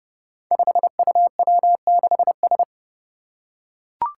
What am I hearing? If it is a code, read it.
Morse code, heard as a keyed tone, about 30 wpm, 700 Hz. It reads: 5UW6S
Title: Five Challenging Letter-Number Combinations 30wpm